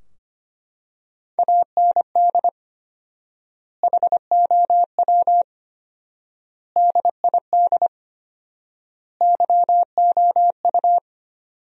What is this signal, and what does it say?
Morse code, heard as a keyed tone, about 25 wpm, 700 Hz. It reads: AND HOW DID YOU
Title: 4x Word-Spacing - Sentences from Top 100 Words - 25wpm